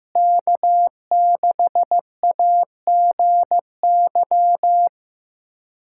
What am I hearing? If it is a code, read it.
Morse code, heard as a keyed tone, about 15 wpm, 700 Hz. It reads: K6AGY
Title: USA Callsign DE Callsign 15wpm